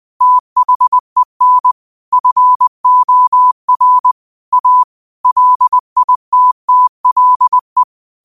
Morse code, heard as a keyed tone, about 20 wpm, 1000 Hz. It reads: THEN FOR A LITTLE